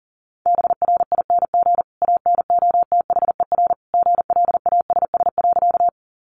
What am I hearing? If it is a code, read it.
Morse code, heard as a keyed tone, about 40 wpm, 700 Hz. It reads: BRING ANOTHER GLASS.
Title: Sentences from Top 1300 Words 40wpm